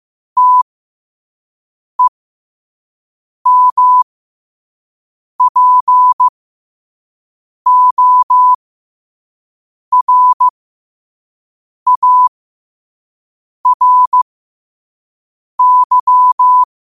Morse code, heard as a keyed tone, about 15 wpm, spaced as for 5 wpm, 1000 Hz. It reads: TEMPORARY
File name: qotdfiles / 5wpm